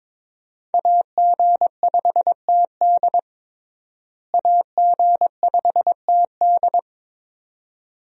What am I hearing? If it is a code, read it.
Morse code, heard as a keyed tone, about 21 wpm, 700 Hz. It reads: AG5TD AG5TD
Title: USA Call Signs - 5 Characters - Repeated 2x - 22wpm